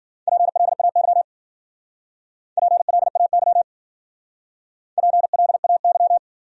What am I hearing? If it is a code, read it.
Morse code, heard as a keyed tone, about 55 wpm, 700 Hz. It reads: PLAY PLAY PLAY